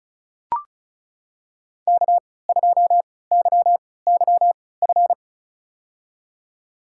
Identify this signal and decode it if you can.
Morse code, heard as a keyed tone, about 34 wpm, spaced as for 20 wpm, 700 Hz. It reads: K2YYF
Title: USA Call Signs - 5 Characters - 3x Character Spacing - 35wpm